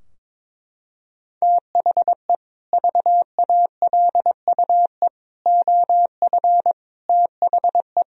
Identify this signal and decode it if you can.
Morse code, heard as a keyed tone, about 21 wpm, 700 Hz. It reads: THE VALUE OF THE
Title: Sentences from Top 1000 Words - Review Part 3 - 22wpm